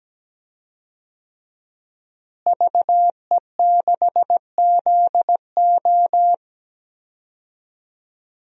Morse code, heard as a keyed tone, about 17 wpm, 700 Hz. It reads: VE6ZO
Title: DXCC-100 Call Signs - 5 Characters - Repeated 2x - 17wpm